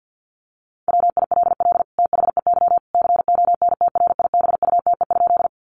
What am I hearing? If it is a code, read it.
Morse code, heard as a keyed tone, about 48 wpm, 700 Hz. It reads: WILL THEY CONTRIBUTE?
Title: Sentences from Top 1400 Words 50wpm